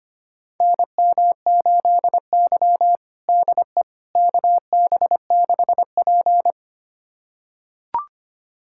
Morse code, heard as a keyed tone, about 25 wpm, 700 Hz. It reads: NM8Y DE KB6P
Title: USA Callsign DE Callsign - 4 Characters - 25wpm